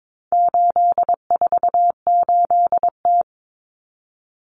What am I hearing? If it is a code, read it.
Morse code, heard as a keyed tone, about 22 wpm, 700 Hz. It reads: 848T
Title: Common Cut Numbers - 4-Digit 22wpm